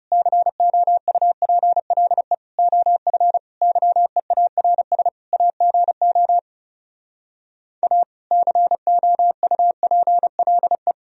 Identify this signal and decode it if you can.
Morse code, heard as a keyed tone, about 40 wpm, 700 Hz. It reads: COUPLE OF YEARS AGO A COUPLE